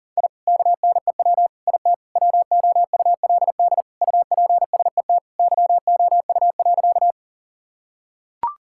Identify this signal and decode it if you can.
Morse code, heard as a keyed tone, about 40 wpm, 700 Hz. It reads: I KNEW IT WOULD UPSET YOU.
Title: Sentences from Top 1000 Words - Review Part 5 - 40wpm